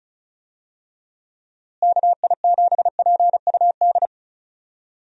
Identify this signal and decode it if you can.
Morse code, heard as a keyed tone, about 36 wpm, 700 Hz. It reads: KI7PUD
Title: USA Call Signs - 6 Characters - Repeated 2x - 35wpm